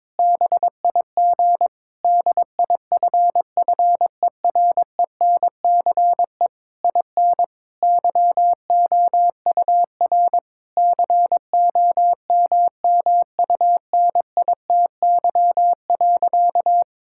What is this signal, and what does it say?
Morse code, heard as a keyed tone, about 22 wpm, 700 Hz. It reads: BIG DIFFERENCE IN YOUR COMMUNITY.
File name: Difficult Sentences with Hints - 22wpm